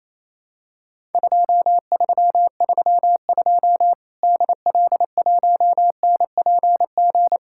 Morse code, heard as a keyed tone, about 27 wpm, 700 Hz. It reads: 2332 DL1NPG
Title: Worked All Europe DX Contest - QTC Exchange - 28wpm